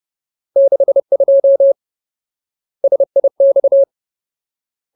Morse code, heard as a keyed tone, about 30 wpm, 550 Hz. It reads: 62 SIX
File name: Ultimate Word List / 30WPM Ultimate